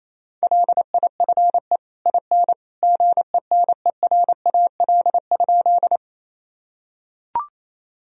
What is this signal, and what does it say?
Morse code, heard as a keyed tone, about 28 wpm, 700 Hz. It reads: LIFE IN GENERAL?
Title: Sentences from Top 1000 Words - Review Part 3 - 28wpm